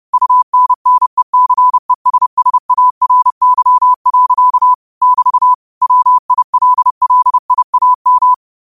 Morse code, heard as a keyed tone, about 30 wpm, 1000 Hz. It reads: UNNECESSARY. <BT> WILLIAM
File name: qotdfiles / 30wpm